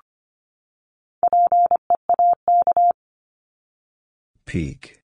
Morse code, heard as a keyed tone, about 25 wpm, 700 Hz. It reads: PEAK
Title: Top 1751-2000 Words 25wpm